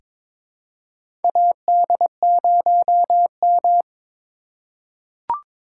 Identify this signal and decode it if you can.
Morse code, heard as a keyed tone, about 22 wpm, 700 Hz. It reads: AD0M
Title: USA Call Signs - 4 Characters - 2x Character Spacing - 22wpm